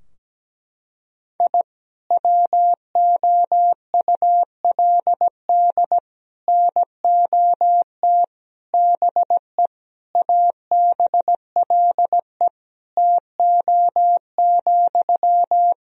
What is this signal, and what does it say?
Morse code, heard as a keyed tone, about 17 wpm, 700 Hz. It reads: I WOULD NOT BE ABLE TO,
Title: Sentences from Top 1000 Words - Review Part 2 - 17wpm